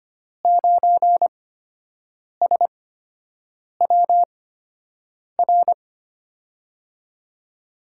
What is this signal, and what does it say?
Morse code, heard as a keyed tone, about 25 wpm, spaced as for 7 wpm, 700 Hz. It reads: 9SWR
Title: USA Call Signs - 5 Characters - 8x Character Spacing - 25wpm